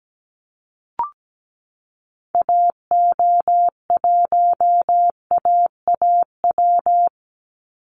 Morse code, heard as a keyed tone, about 17 wpm, 700 Hz. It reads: AO1AAW
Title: Call Signs Repeated 3x - 17wpm